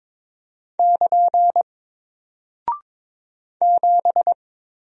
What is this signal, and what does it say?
Morse code, heard as a keyed tone, about 22 wpm, 700 Hz. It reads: <KN> 7
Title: Single Character (Letter:Number:Punctuation:Prosign) Alt 22wpm